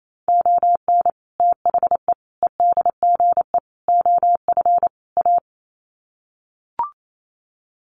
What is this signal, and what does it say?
Morse code, heard as a keyed tone, about 28 wpm, 700 Hz. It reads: ON THE EDGE OF A
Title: Sets of 5 Words Encore2 28wpm